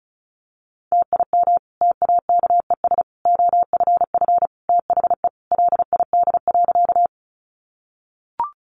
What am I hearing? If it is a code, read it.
Morse code, heard as a keyed tone, about 34 wpm, 700 Hz. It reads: TIM TAKES OFF THE LID.
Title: Easy Phrases 35wpm